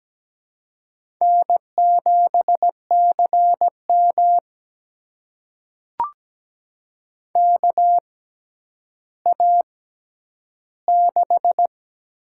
Morse code, heard as a keyed tone, about 17 wpm, spaced as for 10 wpm, 700 Hz. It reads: N7CM KA6
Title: USA Call Signs - 4 Characters - 6x Character Spacing - 17wpm